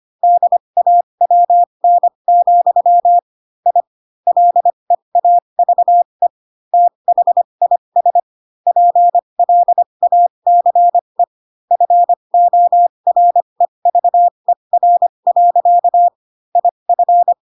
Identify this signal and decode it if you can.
Morse code, heard as a keyed tone, about 25 wpm, spaced as for 22 wpm, 700 Hz. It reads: DAWN, I LEAVE THIS PLACE FOREVER. IF
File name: Pinocchio / Pinocchio ch4 18 25 ik1hge